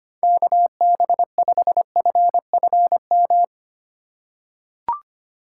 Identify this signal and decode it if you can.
Morse code, heard as a keyed tone, about 25 wpm, 700 Hz. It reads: KB5FFM
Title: USA Call Signs - 6 Characters - 2x Character Spacing - 25wpm